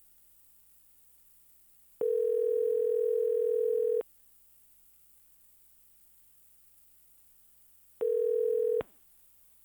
Ringback tone (North America), 440 + 480 Hz.
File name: modem-calling-tone